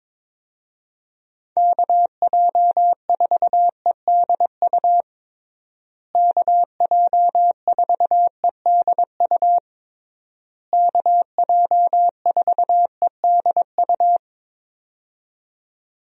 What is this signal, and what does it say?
Morse code, heard as a keyed tone, about 22 wpm, 700 Hz. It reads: KJ4EDU KJ4EDU KJ4EDU
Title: USA Call Signs - 6 Characters - Repeated 3x - 22wpm